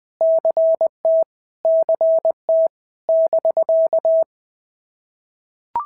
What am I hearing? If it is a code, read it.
Morse code, heard as a keyed tone, about 20 wpm, 650 Hz. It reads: CT CT <BK>
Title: POTA QSO Practice - 20wpm